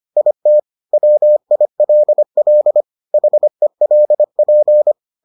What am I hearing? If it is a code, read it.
Morse code, heard as a keyed tone, about 25 wpm, 600 Hz. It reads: IT WILL HELP